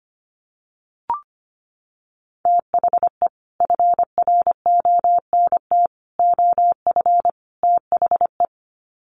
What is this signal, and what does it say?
Morse code, heard as a keyed tone, about 25 wpm, 700 Hz. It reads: THE FRONT OF THE
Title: Sets of 4 Words 25wpm